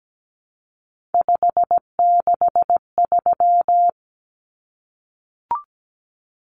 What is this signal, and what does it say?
Morse code, heard as a keyed tone, about 17 wpm, 700 Hz. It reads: 563
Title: Numbers - 3-Digit 17wpm